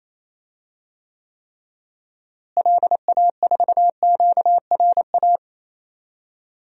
Morse code, heard as a keyed tone, about 28 wpm, 700 Hz. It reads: LA4QRA
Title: DXCC-100 Call Signs - 6 Characters - Repeated 3x - 28wpm